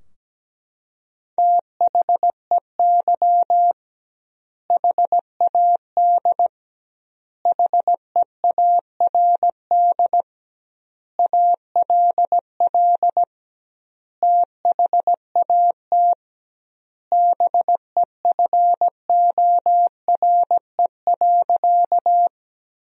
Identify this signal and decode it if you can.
Morse code, heard as a keyed tone, about 17 wpm, 700 Hz. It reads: THEY HAD HEARD ALL THAT BEFORE.
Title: 2x Word-Spacing - Sentences from Top 400 Words - 17wpm